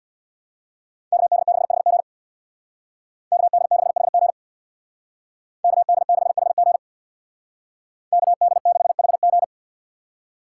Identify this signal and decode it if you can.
Morse code, heard as a keyed tone, about 64 wpm, 700 Hz. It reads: KD6HG KD6HG KD6HG KD6HG